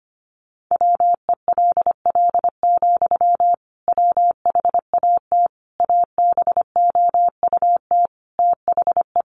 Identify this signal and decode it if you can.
Morse code, heard as a keyed tone, about 25 wpm, 700 Hz. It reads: WELL, WHAT ABOUT THE
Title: Sentences from Top 1200 Words 25wpm